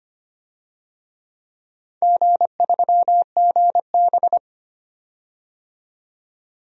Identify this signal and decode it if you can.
Morse code, heard as a keyed tone, about 25 wpm, 700 Hz. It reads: G3GB
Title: DXCC-100 Call Signs - 4 Characters - 25wpm